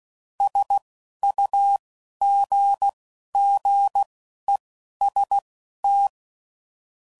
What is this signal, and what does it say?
Morse code, heard as a keyed tone, about 16 wpm, spaced as for 11 wpm, 800 Hz. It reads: SUGGEST
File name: CW Code Groups @10WPM / Technical Subjects / AND Gate